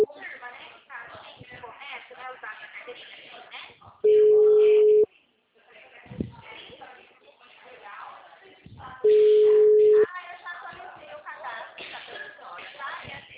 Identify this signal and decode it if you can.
Ringback tone (Europe, 425 Hz), with speech.